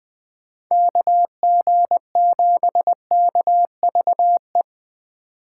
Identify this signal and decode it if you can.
Morse code, heard as a keyed tone, about 22 wpm, 700 Hz. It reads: KG7KVE
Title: Speed Racing - USA 2x3 Call Signs 20wpm